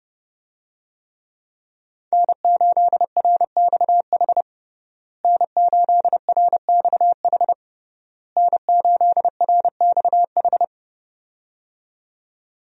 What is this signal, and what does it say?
Morse code, heard as a keyed tone, about 30 wpm, 700 Hz. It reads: N8RXH N8RXH N8RXH
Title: USA Call Signs - 5 Characters - Repeated 3x - 30wpm